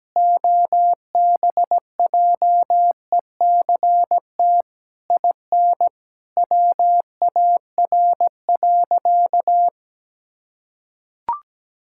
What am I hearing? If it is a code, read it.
Morse code, heard as a keyed tone, about 17 wpm, 700 Hz. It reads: OBJECT IN WAR.
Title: Sentences from Top 1000 Words - Review Part 6 - 17wpm